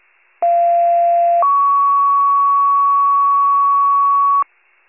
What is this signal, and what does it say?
Two-tone page, heard as a one-second tone, then a three-second tone, about 690 then 1120 Hz, with noise. It is heard from 0.4 to 4.4 s.
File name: pager / 38 FIRE / wcalert